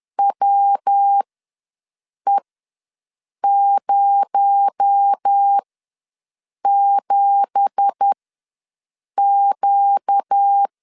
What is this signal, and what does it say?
Morse code, heard as a keyed tone, about 10 wpm, spaced as for 6 wpm, 800 Hz. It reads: WE07Q
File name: LEZIONI / 10 SIGNS / 10D COMA ws 10wpm 4314